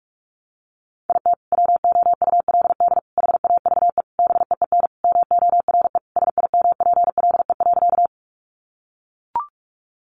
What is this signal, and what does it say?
Morse code, heard as a keyed tone, about 44 wpm, 700 Hz. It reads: IT WOULD HAVE BEEN MORE SIMPLE.
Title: Sentences from Top 400 Words 45wpm